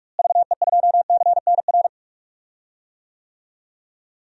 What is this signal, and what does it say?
Morse code, heard as a keyed tone, about 43 wpm, 700 Hz. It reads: VE2CNR
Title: DXCC-100 Call Signs - 6 Characters - Repeated 3x - 45wpm